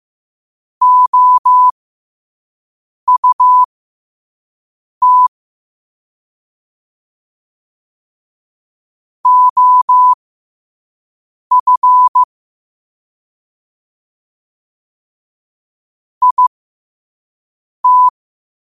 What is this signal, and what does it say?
Morse code, heard as a keyed tone, about 15 wpm, spaced as for 5 wpm, 1000 Hz. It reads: OUT OF IT
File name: qotdfiles / 5wpm